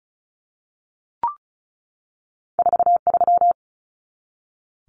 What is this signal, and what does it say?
Morse code, heard as a keyed tone, about 36 wpm, 700 Hz. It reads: 43
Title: Common Cut Numbers - 2-Digit 35wpm